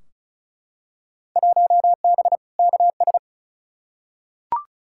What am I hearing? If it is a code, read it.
Morse code, heard as a keyed tone, about 36 wpm, 700 Hz. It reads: 1B KS
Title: Field Day Exchanges Encore - 35wpm